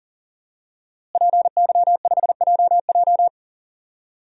Morse code, heard as a keyed tone, about 41 wpm, 700 Hz. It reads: PY5JJ
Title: DXCC-100 Call Signs - 5 Characters - Repeated 2x - 40wpm